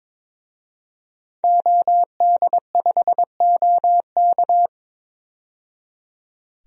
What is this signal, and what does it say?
Morse code, heard as a keyed tone, about 22 wpm, 700 Hz. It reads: OD5OK
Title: DXCC-100 Call Signs - 5 Characters - 22wpm